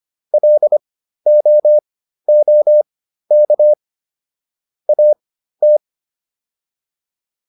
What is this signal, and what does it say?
Morse code, heard as a keyed tone, about 25 wpm, spaced as for 13 wpm, 600 Hz. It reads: LOOK AT
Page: /intermediate-practice-files/